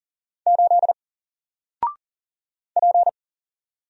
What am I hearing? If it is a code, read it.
Morse code, heard as a keyed tone, about 40 wpm, 700 Hz. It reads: 8 P
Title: Single Character (Letter:Number:Punctuation:Prosign) Alt 40wpm